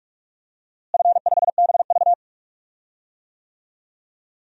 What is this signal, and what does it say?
Morse code, heard as a keyed tone, about 46 wpm, 700 Hz. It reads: F5BV
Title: DXCC-100 Call Signs - 4 Characters - Repeated 2x - 45wpm